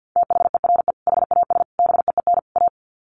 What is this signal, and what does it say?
Morse code, heard as a keyed tone, about 50 wpm, 700 Hz. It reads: THERE HAS BEEN A
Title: Sets of 4 Words 50wpm